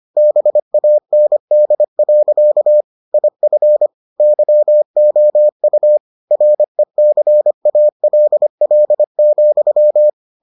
Morse code, heard as a keyed tone, about 25 wpm, 600 Hz. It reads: BAND. IF YOU RECALL,